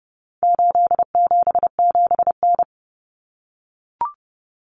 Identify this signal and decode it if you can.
Morse code, heard as a keyed tone, about 30 wpm, 700 Hz. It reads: 877N
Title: Common Cut Numbers - 4-Digit 30wpm